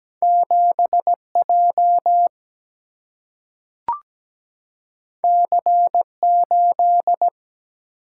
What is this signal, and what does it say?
Morse code, heard as a keyed tone, about 17 wpm, 700 Hz. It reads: 7J C8
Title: Single-Letter Single-Number 17wpm